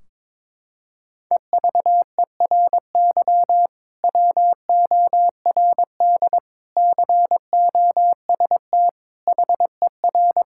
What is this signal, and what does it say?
Morse code, heard as a keyed tone, about 22 wpm, 700 Hz. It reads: EVERY WORD COST HER
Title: Sentences from Top 1600 Words 22wpm